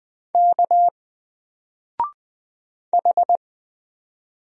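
Morse code, heard as a keyed tone, about 20 wpm, 700 Hz. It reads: K H
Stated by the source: Single letter and number Morse drill at 20 wpm, from N4LCD.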